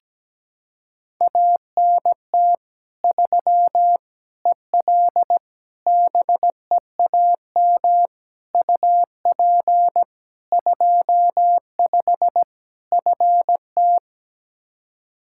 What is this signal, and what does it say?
Morse code, heard as a keyed tone, about 17 wpm, 700 Hz. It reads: ANT 3 EL BEAM UP 25 FT
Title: Practice QSO - Exchange 2 - Info - 17wpm